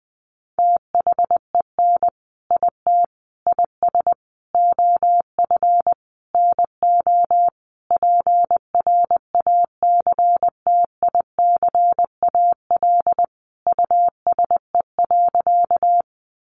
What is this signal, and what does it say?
Morse code, heard as a keyed tone, about 20 wpm, 700 Hz. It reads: THEN IT IS OF NO PRACTICAL USE.
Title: Sentences from Top 1400 Words 20wpm